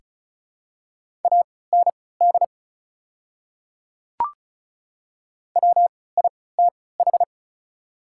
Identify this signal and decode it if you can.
Morse code, heard as a keyed tone, about 36 wpm, spaced as for 20 wpm, 690 Hz. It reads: AND WITH
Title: 3x Character Spacing - Top 200 Words - 35wpm